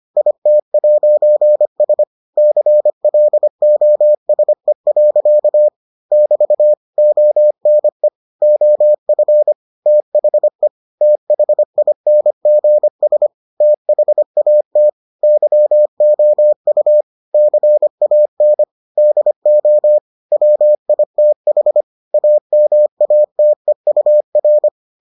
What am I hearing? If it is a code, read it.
Morse code, heard as a keyed tone, about 25 wpm, 600 Hz. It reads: IT'S CLOSE. <BT> ONE OF THE THINGS THAT YOU CAN DO WITH AMATEUR